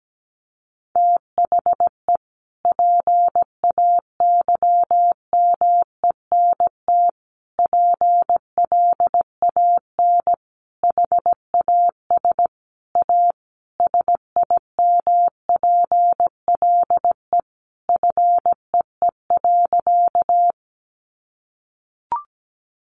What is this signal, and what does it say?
Morse code, heard as a keyed tone, about 17 wpm, 700 Hz. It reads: THE PAYMENT PLAN HAS A SIMPLE FEE.
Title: Sentences from Top 1500 Words 17wpm